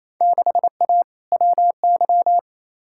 Morse code, heard as a keyed tone, about 28 wpm, 700 Hz. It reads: 6A WY
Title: Field Day Exchanges Encore - 28wpm